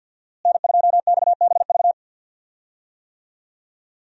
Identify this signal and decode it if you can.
Morse code, heard as a keyed tone, about 51 wpm, 700 Hz. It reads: N2XBV
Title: USA Call Signs - 5 Characters - Repeated 3x - 50wpm